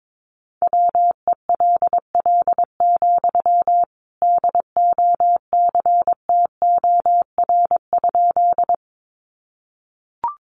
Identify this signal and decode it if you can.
Morse code, heard as a keyed tone, about 22 wpm, 700 Hz. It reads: WELL, DOCTOR?
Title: Sentences from Top 1500 Words 22wpm